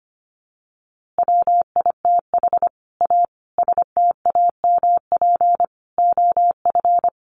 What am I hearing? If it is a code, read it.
Morse code, heard as a keyed tone, about 25 wpm, 700 Hz. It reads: WITH A STAMP OF
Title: A Sound Of Thunder 25wpm